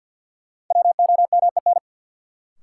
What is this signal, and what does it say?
Morse code, heard as a keyed tone, about 51 wpm, 700 Hz. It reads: WOMEN
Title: Sets of 3 Words - New Word Drill - 50wpm